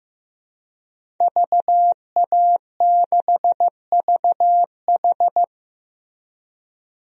Morse code, heard as a keyed tone, about 16 wpm, 700 Hz. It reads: VA6VH
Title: DXCC-100 Call Signs - 5 Characters - Repeated 3x - 15wpm